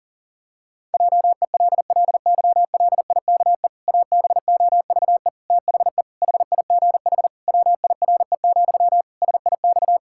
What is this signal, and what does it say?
Morse code, heard as a keyed tone, about 40 wpm, 700 Hz. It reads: JELLYLIKE ABOVE THE HIGH WIRE, SIX
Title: Pangram Challenge v2 - 40wpm